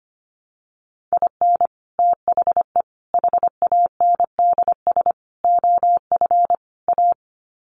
Morse code, heard as a keyed tone, about 25 wpm, 700 Hz. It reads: IN THE HANDS OF A
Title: Sets of 5 Words Encore 25wpm